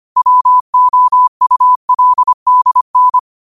Morse code, heard as a keyed tone, about 25 wpm, 1000 Hz. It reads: WOULDN
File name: qotdfiles / 25wpm